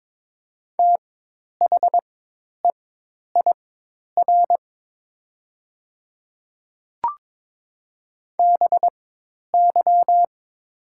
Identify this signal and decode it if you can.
Morse code, heard as a keyed tone, about 22 wpm, spaced as for 10 wpm, 700 Hz. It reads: THEIR BY
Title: 4x Character Spacing - Top 100 Words - 22wpm